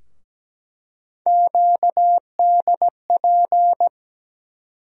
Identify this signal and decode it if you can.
Morse code, heard as a keyed tone, about 17 wpm, 700 Hz. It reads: QDP
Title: USA Call Sign Postfixes - 3 Letters 17wpm